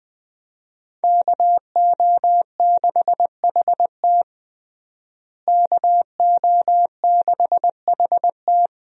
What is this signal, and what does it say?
Morse code, heard as a keyed tone, about 20 wpm, 700 Hz. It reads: KO6HT KO6HT
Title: USA Call Signs - 5 Characters - Repeated 2x - 20wpm